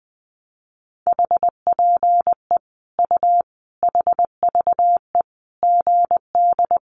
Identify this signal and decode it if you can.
Morse code, heard as a keyed tone, about 20 wpm, 700 Hz. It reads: HPE U HVE GD
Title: Common QSO Phrases 20wpm